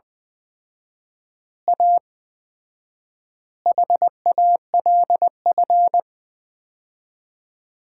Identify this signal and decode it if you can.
Morse code, heard as a keyed tone, about 20 wpm, 700 Hz. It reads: A HALF
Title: 4x Word-Spacing - Sets of 5 Words - 20wpm